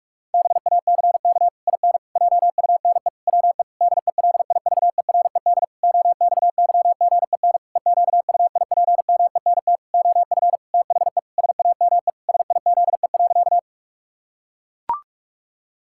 Morse code, heard as a keyed tone, about 41 wpm, 700 Hz. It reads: BACK IN JUNE WE DELIVERED OXYGEN EQUIPMENT OF THE SAME SIZE.
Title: Pangram Challenge v2 - 45wpm